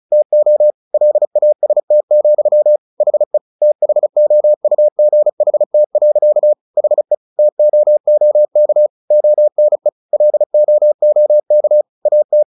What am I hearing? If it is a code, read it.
Morse code, heard as a keyed tone, about 34 wpm, 600 Hz. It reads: TO LAST, HE THOUGHT. HE TOOK ONE LOOK AT
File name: the old man and the sea 2 35wpm